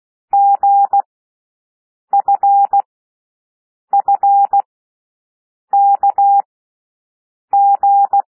Morse code, heard as a keyed tone, about 16 wpm, spaced as for 6 wpm, 800 Hz. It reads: GFFKG
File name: cwmp3 / L1 SOUND LETTERS 16 wpm